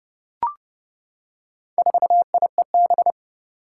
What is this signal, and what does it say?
Morse code, heard as a keyed tone, about 30 wpm, 700 Hz. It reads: 4IEB
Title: Four Challenging Letter-Number Combinations 30wpm